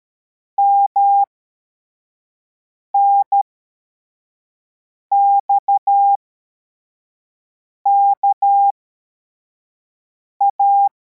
Morse code, heard as a keyed tone, about 13 wpm, spaced as for 4 wpm, 800 Hz. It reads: MNXKA
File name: formation / cw / Text25